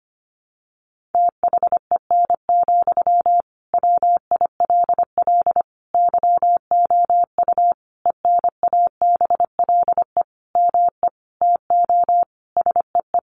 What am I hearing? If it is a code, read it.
Morse code, heard as a keyed tone, about 25 wpm, 700 Hz. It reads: THEN, WILL YOU ENABLE ME TO SEE
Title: Sentences from Top 1500 Words 25wpm